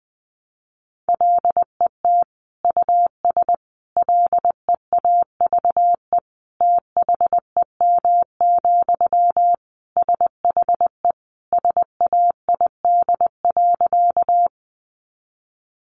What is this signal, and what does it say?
Morse code, heard as a keyed tone, about 20 wpm, 700 Hz. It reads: LET US LEAVE THEM, SHE SAID.
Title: Sentences from Top 400 Words 20wpm